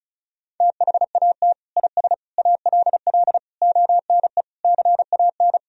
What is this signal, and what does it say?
Morse code, heard as a keyed tone, about 36 wpm, 700 Hz. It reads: THAT IS ALL ONE CAN
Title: Sentences from Top 200 Words and Abbreviations 35wpm